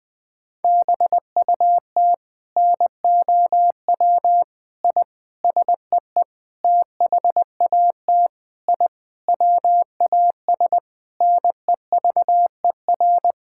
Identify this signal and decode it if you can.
Morse code, heard as a keyed tone, about 20 wpm, 700 Hz. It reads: BUT NOW I SEE THAT I WAS NEVER
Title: Sentences from Top 1000 Words - Review Part 6 - 20wpm